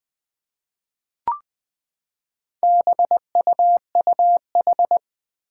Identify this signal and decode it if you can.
Morse code, heard as a keyed tone, about 20 wpm, 700 Hz. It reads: BUUH
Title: Four Challenging Letter-Number Combinations 20wpm